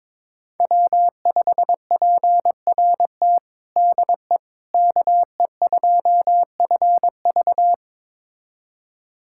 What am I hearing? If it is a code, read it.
Morse code, heard as a keyed tone, about 22 wpm, 700 Hz. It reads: W5PRT DE KE2FV
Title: USA Callsign DE Callsign - 5 Characters - 22wpm